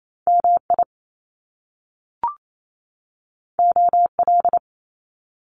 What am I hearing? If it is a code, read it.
Morse code, heard as a keyed tone, about 28 wpm, 700 Hz. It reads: MI OL
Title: Top Two Letter Combinations in Words 28wpm